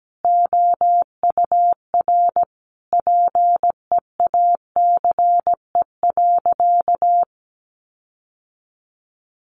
Morse code, heard as a keyed tone, about 17 wpm, 700 Hz. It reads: OUR PEACE.
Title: Sentences from Top 1500 Words 17wpm